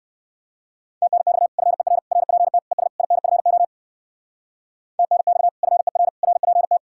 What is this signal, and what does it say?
Morse code, heard as a keyed tone, about 58 wpm, 700 Hz. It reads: TNX FER RPT ES INFO TNX FER RPT